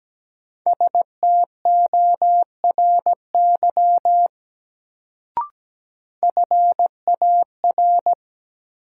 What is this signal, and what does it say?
Morse code, heard as a keyed tone, about 17 wpm, 700 Hz. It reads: STORY FAR
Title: Sentences from Top 300 Words - New Word Drill - 17wpm